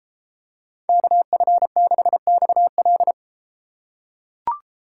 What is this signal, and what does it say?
Morse code, heard as a keyed tone, about 32 wpm, 700 Hz. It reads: KF6XL